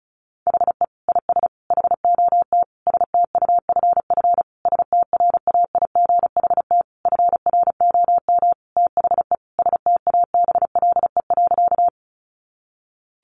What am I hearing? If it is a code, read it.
Morse code, heard as a keyed tone, about 34 wpm, 700 Hz. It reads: HE IS HOT STUFF STRAIGHT FROM THE STABLE.
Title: Sentences from Top 1500 Words 35wpm